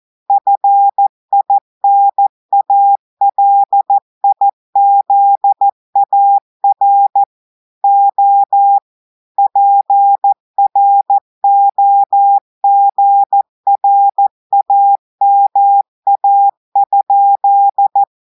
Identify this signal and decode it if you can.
Morse code, heard as a keyed tone, about 14 wpm, 800 Hz. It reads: FINALIZAR O PROGRAMA?
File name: audios / portugais / portugais texte en clair 2 vitesse 14 mots minutes